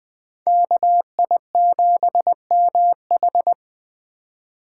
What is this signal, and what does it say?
Morse code, heard as a keyed tone, about 20 wpm, 700 Hz. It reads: KI7MH
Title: Speed Racing - USA 2x2 Call Signs 17wpm